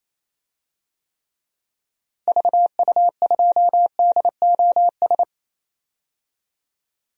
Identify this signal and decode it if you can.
Morse code, heard as a keyed tone, about 28 wpm, 700 Hz. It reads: VU2DOS
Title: DXCC-100 Call Signs - 6 Characters - Repeated 2x - 28wpm